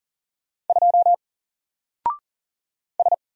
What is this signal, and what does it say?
Morse code, heard as a keyed tone, about 40 wpm, 700 Hz. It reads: 2 S
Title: Single Character (Letter:Number:Punctuation:Prosign) 40wpm